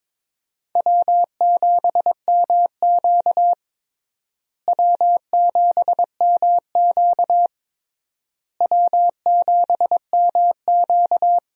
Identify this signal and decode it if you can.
Morse code, heard as a keyed tone, about 22 wpm, 700 Hz. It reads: W7MQ W7MQ W7MQ
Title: USA Call Signs - 4 Characters - Repeated 3x - 22wpm